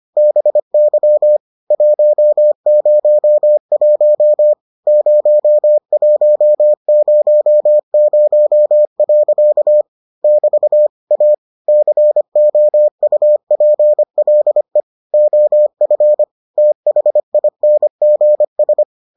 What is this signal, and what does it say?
Morse code, heard as a keyed tone, about 25 wpm, 600 Hz. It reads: BY 101 0100. <BT> A COUPLE OF THINGS